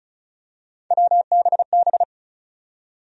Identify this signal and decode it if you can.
Morse code, heard as a keyed tone, about 34 wpm, 700 Hz. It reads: WBB
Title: USA Call Sign Postfixes - 3 Letters 35wpm